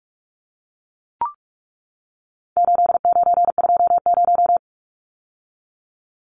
Morse code, heard as a keyed tone, about 44 wpm, 700 Hz. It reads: 8920
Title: Numbers - 4-Digit 45wpm